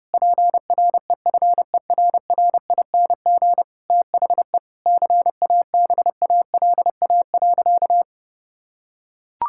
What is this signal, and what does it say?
Morse code, heard as a keyed tone, about 32 wpm, 700 Hz. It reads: PREFERRING THE CABALA.
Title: Pangram Challenge v2 - 30wpm